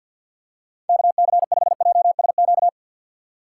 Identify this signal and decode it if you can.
Morse code, heard as a keyed tone, about 53 wpm, 700 Hz. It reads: KC5JSQ
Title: USA Call Signs - 6 Characters - 3x Character Spacing - 50wpm